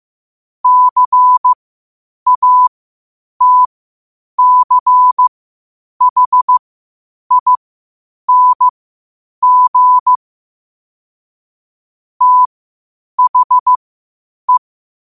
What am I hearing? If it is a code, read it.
Morse code, heard as a keyed tone, about 15 wpm, spaced as for 8 wpm, 1000 Hz. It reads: CATCHING THE
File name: qotdfiles / 8wpm